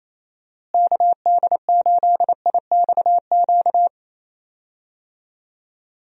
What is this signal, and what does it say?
Morse code, heard as a keyed tone, about 28 wpm, 700 Hz. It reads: KD8IXQ
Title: USA Call Signs - 6 Characters - Repeated 2x - 28wpm